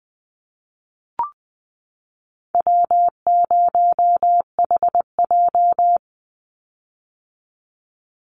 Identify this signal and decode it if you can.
Morse code, heard as a keyed tone, about 20 wpm, 700 Hz. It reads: W0HJ
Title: USA Call Signs 20wpm